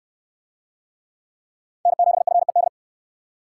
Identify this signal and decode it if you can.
Morse code, heard as a keyed tone, about 69 wpm, 700 Hz. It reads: N7FED